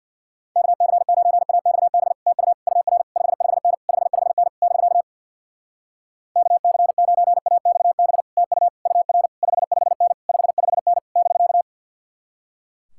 Morse code, heard as a keyed tone, about 65 wpm, 700 Hz. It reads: KC9AXB TU UR 55N 55N <BK> KC9AXB TU UR 55N 55N <BK>